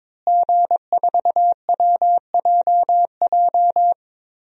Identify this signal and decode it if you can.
Morse code, heard as a keyed tone, about 22 wpm, 700 Hz. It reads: G4WJJ
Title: DXCC-100 Call Signs - 5 Characters - Repeated 3x - 22wpm